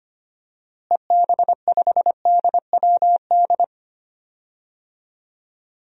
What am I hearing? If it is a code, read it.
Morse code, heard as a keyed tone, about 25 wpm, 700 Hz. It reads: EB5DWD
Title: DXCC-100 Call Signs - 6 Characters - Repeated 3x - 25wpm